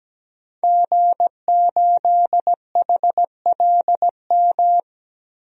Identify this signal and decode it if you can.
Morse code, heard as a keyed tone, about 17 wpm, 700 Hz. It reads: G8HLM
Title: DXCC-100 Call Signs - 5 Characters - 17wpm